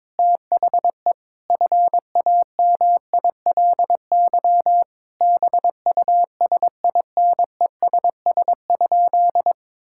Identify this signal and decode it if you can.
Morse code, heard as a keyed tone, about 22 wpm, 700 Hz. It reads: THE FAMILY BUSINESS?
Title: Sentences from Top 1000 Words - Review Part 6 - 22wpm